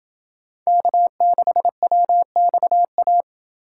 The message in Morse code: K6WXA

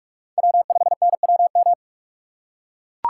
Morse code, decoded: W5NWM

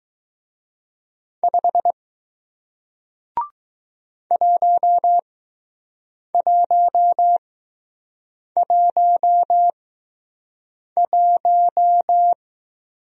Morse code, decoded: 5 1111

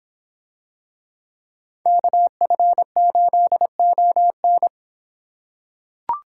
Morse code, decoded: KF8ON

